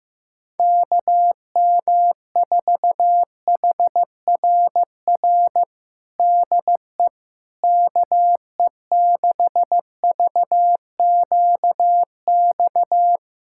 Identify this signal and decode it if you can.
Morse code, heard as a keyed tone, about 15 wpm, 700 Hz. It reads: KM4HRR DE KE6VQX
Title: USA Callsign DE Callsign 15wpm